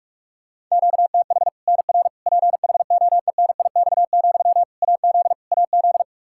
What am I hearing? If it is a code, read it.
Morse code, heard as a keyed tone, about 45 wpm, 700 Hz. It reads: QTH NR PHOENIX, AZ AZ
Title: K1USN SST QSO Practice 45wpm